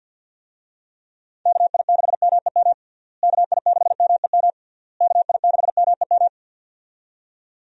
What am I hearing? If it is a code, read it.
Morse code, heard as a keyed tone, about 51 wpm, 700 Hz. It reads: KI6MEM KI6MEM KI6MEM
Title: USA Call Signs - 6 Characters - Repeated 3x - 50wpm